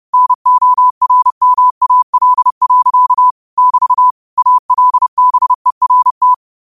Morse code, heard as a keyed tone, about 30 wpm, 1000 Hz. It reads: NORMAL. <BT> ALBERT
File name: qotdfiles / 30wpm